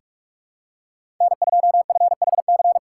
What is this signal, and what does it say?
Morse code, heard as a keyed tone, about 46 wpm, 700 Hz. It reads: N2FHC
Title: USA Call Signs - 5 Characters - 2x Character Spacing - 45wpm